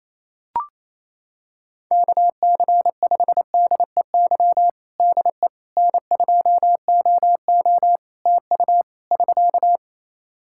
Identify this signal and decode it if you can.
Morse code, heard as a keyed tone, about 28 wpm, 700 Hz. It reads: KC5DEY DE N2OO TU <SK>
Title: K1USN SST QSO Practice 28wpm